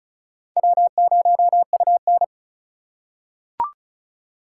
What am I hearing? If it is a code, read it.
Morse code, heard as a keyed tone, about 34 wpm, 700 Hz. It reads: W0UN
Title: USA Call Signs - 4 Characters - 4x Character Spacing - 35wpm